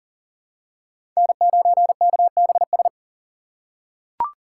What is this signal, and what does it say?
Morse code, heard as a keyed tone, about 40 wpm, 700 Hz. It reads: N9KBS
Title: USA Call Signs - 2x Character Spacing - 40wpm